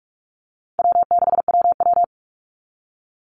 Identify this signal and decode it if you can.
Morse code, heard as a keyed tone, about 44 wpm, 700 Hz. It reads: W6WW
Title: NA SOTA Chaser Callsigns 45wpm